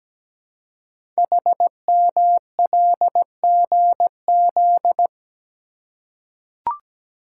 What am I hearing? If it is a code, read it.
Morse code, heard as a keyed tone, about 17 wpm, 700 Hz. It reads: HMLGZ
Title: Code Groups - 5 Letters - 17wpm